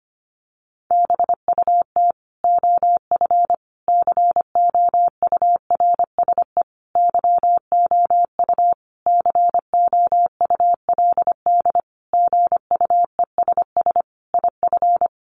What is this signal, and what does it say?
Morse code, heard as a keyed tone, about 25 wpm, 700 Hz. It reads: BUT OF COURSE YOU COULD GUESS IF
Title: Sentences from Top 1500 Words 25wpm